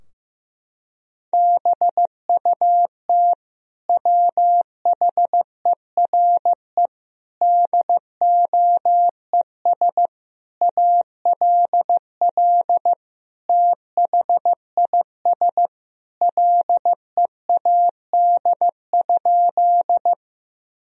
Morse code, heard as a keyed tone, about 15 wpm, 700 Hz. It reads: BUT WHERE DOES ALL THIS LEAD?
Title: Sentences from Top 1000 Words - Review Part 2 - 15wpm